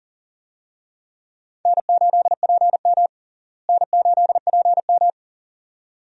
Morse code, heard as a keyed tone, about 37 wpm, 700 Hz. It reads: N8PM N8PM
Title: USA Call Signs - 4 Characters - Repeated 2x - 40wpm